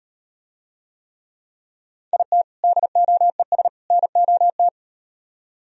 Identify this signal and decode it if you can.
Morse code, heard as a keyed tone, about 37 wpm, 700 Hz. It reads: IT DOES NOT